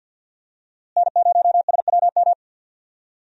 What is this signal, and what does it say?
Morse code, heard as a keyed tone, about 51 wpm, 700 Hz. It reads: N0SWM
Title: USA Call Signs - 5 Characters - Repeated 2x - 50wpm